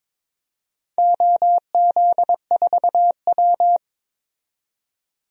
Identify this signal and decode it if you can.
Morse code, heard as a keyed tone, about 22 wpm, 700 Hz. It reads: OZ4W
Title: DXCC-100 Call Signs - 4 Characters - 22wpm